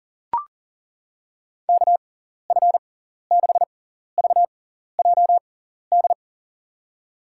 Morse code, heard as a keyed tone, about 40 wpm, spaced as for 14 wpm, 700 Hz. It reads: KF6VJD
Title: USA Call Signs - 6 Characters - 6x Character Spacing - 40wpm